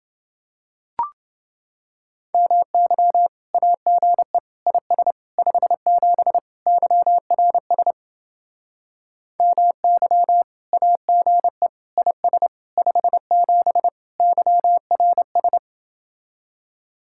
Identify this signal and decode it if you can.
Morse code, heard as a keyed tone, about 31 wpm, 700 Hz. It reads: MY AGE IS 57 YRS MY AGE IS 57 YRS